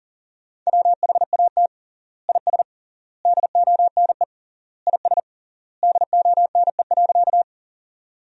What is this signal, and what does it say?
Morse code, heard as a keyed tone, about 40 wpm, 700 Hz. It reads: WHAT IS DONE IS DONE.
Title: 3x Word-Spacing - Sentences from Top 500 Words - 40wpm